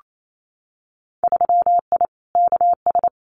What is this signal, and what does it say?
Morse code, heard as a keyed tone, about 28 wpm, 700 Hz. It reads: 3I KS
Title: Winter Field Day Exchanges 28wpm